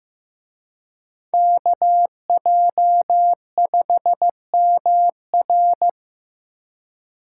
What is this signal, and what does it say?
Morse code, heard as a keyed tone, about 18 wpm, 700 Hz. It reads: KJ5MR
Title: Speed Racing - USA 2x2 Call Signs 15wpm